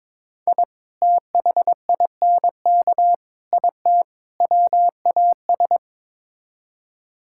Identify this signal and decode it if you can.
Morse code, heard as a keyed tone, about 22 wpm, 700 Hz. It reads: I THINK IT WAS
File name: practice / 22wpm / Build A Phrase - 5-Word Phrases - 22wpm